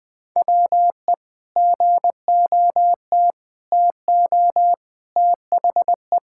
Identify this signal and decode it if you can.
Morse code, heard as a keyed tone, about 20 wpm, 700 Hz. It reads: WE GOT TO THE
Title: Sentences with 5-Word Phrases - Spoken Beginning - 20wpm